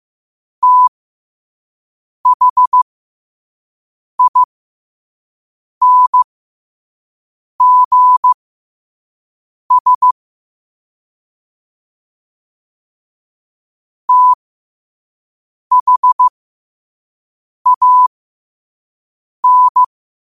Morse code, heard as a keyed tone, about 15 wpm, spaced as for 5 wpm, 1000 Hz. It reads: THINGS THAN